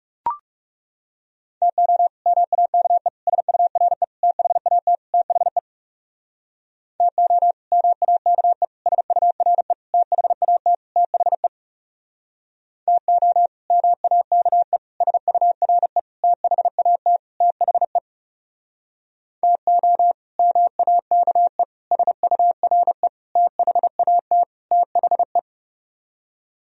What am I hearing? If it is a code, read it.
Morse code, heard as a keyed tone, about 40 wpm, 700 Hz. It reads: TO MAKE SURE THAT THE TO MAKE SURE THAT THE TO MAKE SURE THAT THE TO MAKE SURE THAT THE